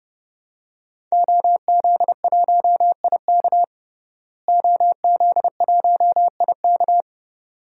Morse code, heard as a keyed tone, about 32 wpm, 700 Hz. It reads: OZ1IK OZ1IK
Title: DXCC-100 Call Signs - 5 Characters - Repeated 2x - 30wpm